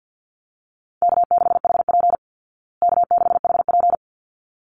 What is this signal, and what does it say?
Morse code, heard as a keyed tone, about 50 wpm, 700 Hz. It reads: K6HP K6HP
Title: Easy Call Signs Repeated 2x - 50wpm